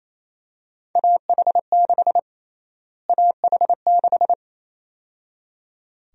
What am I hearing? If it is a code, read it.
Morse code, heard as a keyed tone, about 28 wpm, 700 Hz. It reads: AH6 AH6
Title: USA Call Sign Prefixes - Repeated 2x - 28wpm